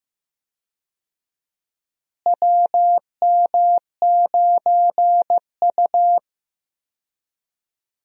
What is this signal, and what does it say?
Morse code, heard as a keyed tone, about 15 wpm, 700 Hz. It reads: WM9U
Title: USA Call Signs - 4 Characters - Repeated 2x - 15wpm